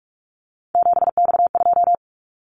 Morse code, heard as a keyed tone, about 46 wpm, 700 Hz. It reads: 7X2
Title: Callsign Prefixes for Top 100 DXCC - 45wpm